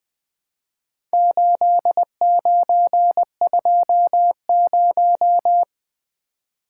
Morse code, heard as a keyed tone, about 20 wpm, 700 Hz. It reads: 8920
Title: Four Letter-Number Combinations - 20wpm